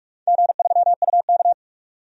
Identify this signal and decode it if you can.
Morse code, heard as a keyed tone, about 46 wpm, 700 Hz. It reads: G3UK
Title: DXCC-100 Call Signs - 4 Characters - 45wpm